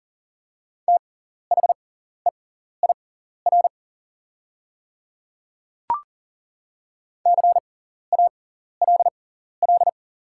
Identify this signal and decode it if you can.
Morse code, heard as a keyed tone, about 40 wpm, spaced as for 14 wpm, 700 Hz. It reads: THEIR CALL
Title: 6x Character Spacing - Top 100 Words - 40wpm